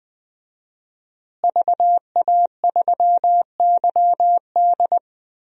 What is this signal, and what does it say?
Morse code, heard as a keyed tone, about 21 wpm, 700 Hz. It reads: VA3YD
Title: DXCC-100 Call Signs - 5 Characters - Repeated 2x - 20wpm